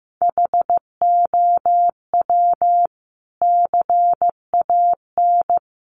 Morse code, heard as a keyed tone, about 15 wpm, 700 Hz. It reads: HOW CAN
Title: Sentences from Top 1200 Words 15wpm